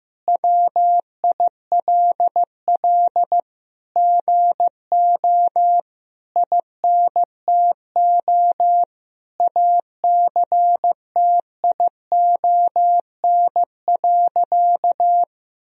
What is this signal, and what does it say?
Morse code, heard as a keyed tone, about 15 wpm, 700 Hz. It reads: WILL GO INTO ACTION.
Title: Sentences from Top 1000 Words - Review Part 4 - 15wpm